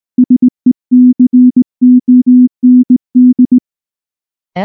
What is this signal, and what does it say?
Morse code, heard as a keyed tone, about 19 wpm, 250 Hz. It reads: SECOND
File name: morse / morse-training-20WPM-3